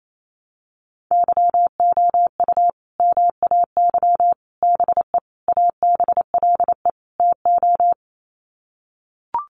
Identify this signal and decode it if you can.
Morse code, heard as a keyed tone, about 28 wpm, 700 Hz. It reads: YOU MAY BE ABLE TO
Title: Sets of 5 Words Encore 28wpm